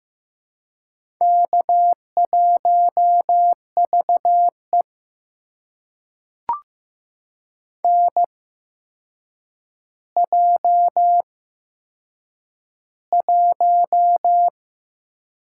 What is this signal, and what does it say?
Morse code, heard as a keyed tone, about 15 wpm, spaced as for 7 wpm, 700 Hz. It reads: K1VE NJ1
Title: USA Call Signs - 4 Characters - 8x Character Spacing - 15wpm